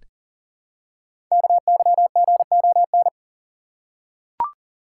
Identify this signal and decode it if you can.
Morse code, heard as a keyed tone, about 39 wpm, 700 Hz. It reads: KYGON
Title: Code Groups - 5 Letters - 40wpm